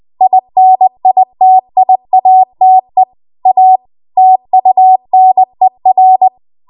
Morse code, heard as a keyed tone, about 20 wpm, 750 Hz. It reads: INITIATE A TUNER
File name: Archive / 20 WPM / 20WPM